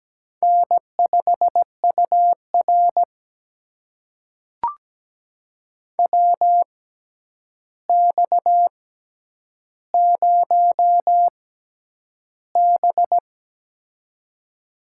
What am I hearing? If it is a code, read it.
Morse code, heard as a keyed tone, about 17 wpm, spaced as for 9 wpm, 700 Hz. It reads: N5UR WX0B
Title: USA Call Signs - 4 Characters - 6x Character Spacing - 17wpm